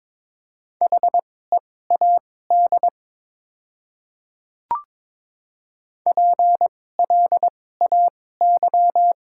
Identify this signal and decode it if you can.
Morse code, heard as a keyed tone, about 22 wpm, spaced as for 16 wpm, 700 Hz. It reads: HEAD PLAY
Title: 2x Character Spacing - Top 300 Words - 22wpm